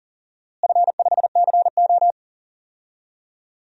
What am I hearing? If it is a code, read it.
Morse code, heard as a keyed tone, about 40 wpm, 700 Hz. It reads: F5CO
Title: DXCC-100 Call Signs - 4 Characters - Repeated 2x - 40wpm